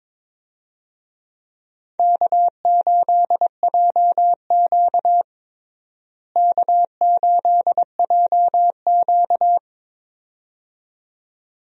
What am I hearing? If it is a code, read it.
Morse code, heard as a keyed tone, about 22 wpm, 700 Hz. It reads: K8JQ K8JQ
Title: USA Call Signs - 4 Characters - Repeated 2x - 22wpm